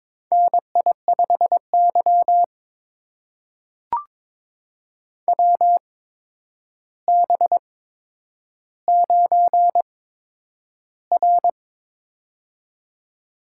Morse code, heard as a keyed tone, about 22 wpm, spaced as for 9 wpm, 700 Hz. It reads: NI5Y WB9R
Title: USA Call Signs - 8x Character Spacing - 22wpm